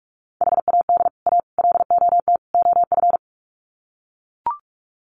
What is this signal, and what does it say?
Morse code, heard as a keyed tone, about 44 wpm, 700 Hz. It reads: HAD A LOT OF
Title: Sets of 4 Words 45wpm